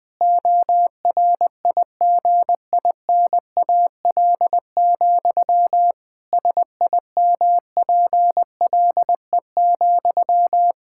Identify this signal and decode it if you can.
Morse code, heard as a keyed tone, about 21 wpm, 700 Hz. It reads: ORIGINAL, SIMPLE,
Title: Sentences from Top 1000 Words - Review Part 6 - 20wpm